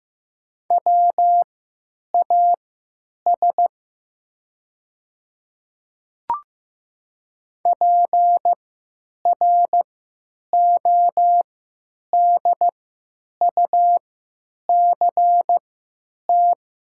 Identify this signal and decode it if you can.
Morse code, heard as a keyed tone, about 15 wpm, spaced as for 9 wpm, 700 Hz. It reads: WAS PRODUCT
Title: 3x Character Spacing - Top 500 Words - 15wpm